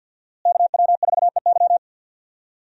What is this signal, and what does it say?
Morse code, heard as a keyed tone, about 49 wpm, 700 Hz. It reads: KW4EY
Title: USA Call Signs - 5 Characters - Repeated 3x - 50wpm